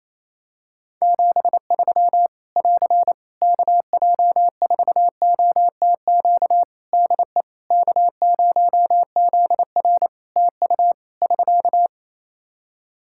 Morse code, heard as a keyed tone, about 27 wpm, 700 Hz. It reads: 73 <AR> KJ4OTQ DE K0ZR TU <SK>
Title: Practice QSO - Exchange 4 - Ending - 28wpm